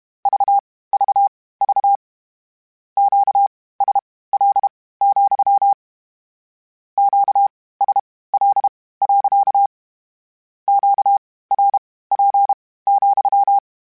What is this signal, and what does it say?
Morse code, heard as a keyed tone, about 32 wpm, spaced as for 18 wpm, 800 Hz. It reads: VVV QSL, QSL. QRP,
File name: Q Calls / Q Calls-mp3s / Q Calls sppc @30